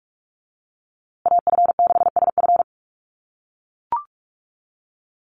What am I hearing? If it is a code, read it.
Morse code, heard as a keyed tone, about 44 wpm, 700 Hz. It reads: AF6SF
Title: USA Call Signs Repeated 3x 45wpm